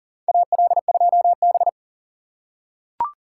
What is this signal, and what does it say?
Morse code, heard as a keyed tone, about 40 wpm, 700 Hz. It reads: AL2B
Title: USA Call Signs - 4 Characters - 6x Character Spacing - 40wpm